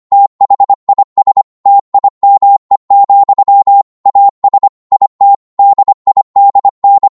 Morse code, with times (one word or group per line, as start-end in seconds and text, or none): 0.1-1.4 THIS
1.7-3.8 TIME,
4.1-4.7 AS
4.9-5.3 IT
5.6-7.1 DIDN